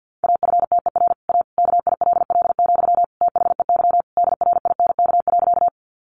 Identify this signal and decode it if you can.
Morse code, heard as a keyed tone, about 50 wpm, 700 Hz. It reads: AFTER A KILL, THEY DRINK.
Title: Sentences from Top 1500 Words 50wpm